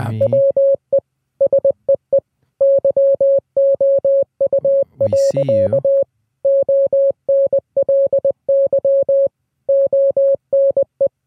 Morse code, heard as a keyed tone, about 20 wpm, 550 Hz. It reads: WE SEE YOU. ONLY ONE